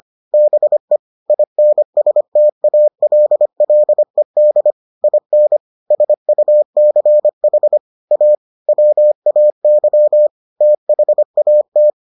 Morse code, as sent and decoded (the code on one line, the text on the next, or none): -... / .. -. ... - .- .-.. .-.. -.. / .. -. / ... ..- -.-. .... / .- / .-- .- -.-- / - .... .- -
BE INSTALLED IN SUCH A WAY THAT